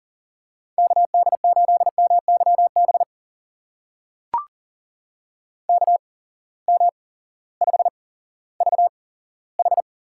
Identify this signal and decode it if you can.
Morse code, heard as a keyed tone, about 40 wpm, spaced as for 18 wpm, 700 Hz. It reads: KD8MYB KM5VH
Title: USA Call Signs - 8x Character Spacing - 40wpm